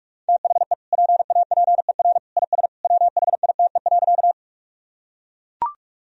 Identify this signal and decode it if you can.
Morse code, heard as a keyed tone, about 41 wpm, 700 Hz. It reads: THE PAPER IS WHITE.
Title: Sentences from Top 1000 Words - Review Part 1 - 45wpm